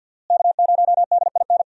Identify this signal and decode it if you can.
Morse code, heard as a keyed tone, about 52 wpm, 700 Hz. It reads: K0DIN
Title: USA Call Signs - 5 Characters - Repeated 3x - 50wpm